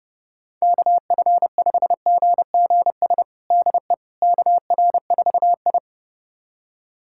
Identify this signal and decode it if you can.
Morse code, heard as a keyed tone, about 30 wpm, 700 Hz. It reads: KF5GGS DE KR4I
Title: USA Callsign DE Callsign 30wpm